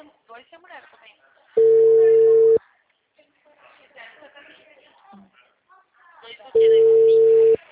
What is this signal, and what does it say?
Ringback tone (China, 450 Hz), with speech.
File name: out- -2021